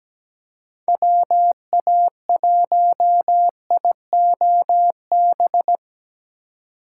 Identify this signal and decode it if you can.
Morse code, heard as a keyed tone, about 17 wpm, 700 Hz. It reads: WA1IOB
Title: USA Call Signs - 6 Characters - 3x Character Spacing - 17wpm